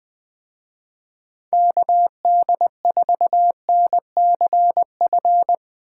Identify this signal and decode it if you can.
Morse code, heard as a keyed tone, about 20 wpm, 700 Hz. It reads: KD4NCF